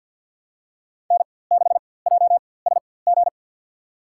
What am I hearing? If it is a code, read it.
Morse code, heard as a keyed tone, about 50 wpm, spaced as for 23 wpm, 700 Hz. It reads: N6JSG
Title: USA Call Signs - 5 Characters - 4x Character Spacing - 50wpm